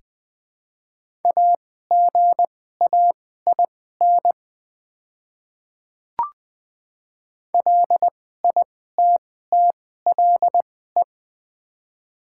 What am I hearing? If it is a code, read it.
Morse code, heard as a keyed tone, about 20 wpm, spaced as for 14 wpm, 700 Hz. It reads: AGAIN LITTLE
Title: 2x Character Spacing - Top 200 Words - 20wpm